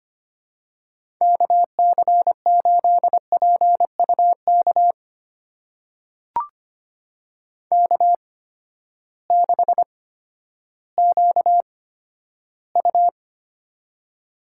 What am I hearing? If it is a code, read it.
Morse code, heard as a keyed tone, about 25 wpm, spaced as for 12 wpm, 700 Hz. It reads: KC8PUK K6QU
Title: USA Call Signs - 8x Character Spacing - 25wpm